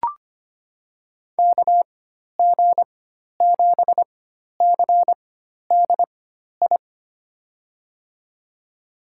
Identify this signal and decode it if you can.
Morse code, heard as a keyed tone, about 25 wpm, spaced as for 12 wpm, 700 Hz. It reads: KG7CDI
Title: USA Call Signs - 6 Characters - 4x Character Spacing - 25wpm